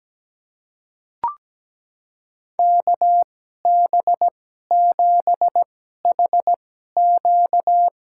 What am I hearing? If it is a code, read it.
Morse code, heard as a keyed tone, about 17 wpm, spaced as for 12 wpm, 700 Hz. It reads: KB7HQ
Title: USA Call Signs - 2x Character Spacing - 17wpm